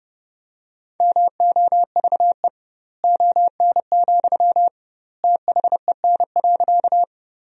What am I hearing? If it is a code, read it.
Morse code, heard as a keyed tone, about 32 wpm, 700 Hz. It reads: MOVE ON, THEN.
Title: 2x Word-Spacing - Sentences from Top 200 Words - 30wpm